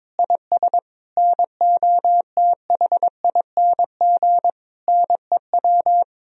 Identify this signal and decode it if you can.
Morse code, heard as a keyed tone, about 22 wpm, 700 Hz. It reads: IS NOTHING NEW
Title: Sentences from Top 1000 Words - Review Part 1 - 22wpm